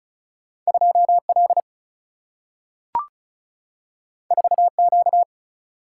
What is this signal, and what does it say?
Morse code, heard as a keyed tone, about 36 wpm, 700 Hz. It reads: 2 L 4 Q
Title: Single-Letter Single-Number 35wpm